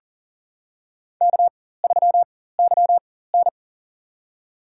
Morse code, heard as a keyed tone, about 40 wpm, spaced as for 19 wpm, 700 Hz. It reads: K3YN
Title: USA Call Signs - 4 Characters - 4x Character Spacing - 40wpm